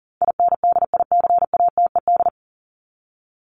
Morse code, heard as a keyed tone, about 40 wpm, 700 Hz. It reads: INDICATED
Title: ED Suffix - Top 100 Words with Base Words 40wpm